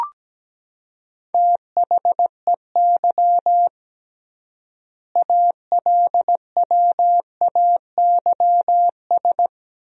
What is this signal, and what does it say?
Morse code, heard as a keyed tone, about 17 wpm, 700 Hz. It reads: THEY ALWAYS
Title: 3x Word-Spacing - Sentences from Top 300 Words - 17wpm